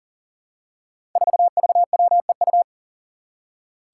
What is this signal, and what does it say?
Morse code, heard as a keyed tone, about 37 wpm, 700 Hz. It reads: 4VWEU
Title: Five Challenging Letter-Number Combinations 40wpm